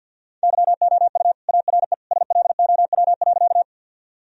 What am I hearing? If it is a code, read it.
Morse code, heard as a keyed tone, about 52 wpm, 700 Hz. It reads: YOU ARE SLOW.
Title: Sentences from Top 1000 Words - Review Part 2 - 50wpm